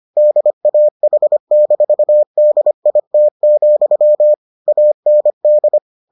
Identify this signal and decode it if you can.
Morse code, heard as a keyed tone, about 25 wpm, 600 Hz. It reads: DAH-DIT, AND